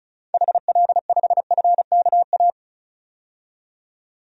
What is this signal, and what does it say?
Morse code, heard as a keyed tone, about 36 wpm, 700 Hz. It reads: HL5FKA
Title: DXCC-100 Call Signs - 6 Characters - 35wpm